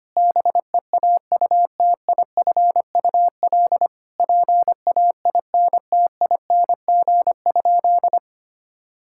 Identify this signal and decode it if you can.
Morse code, heard as a keyed tone, about 25 wpm, 700 Hz. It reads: BEAUTIFUL PAINTING?
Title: Sentences from Top 1000 Words - Review Part 6 - 25wpm